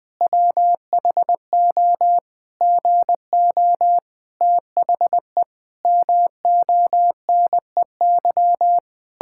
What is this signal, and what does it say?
Morse code, heard as a keyed tone, about 20 wpm, 700 Hz. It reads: WHO GO THE MONEY
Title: Pangram Challenge v2 - 20wpm